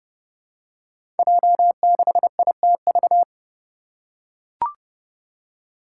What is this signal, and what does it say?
Morse code, heard as a keyed tone, about 30 wpm, 700 Hz. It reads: J6ITV
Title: Five Challenging Letter-Number Combinations 30wpm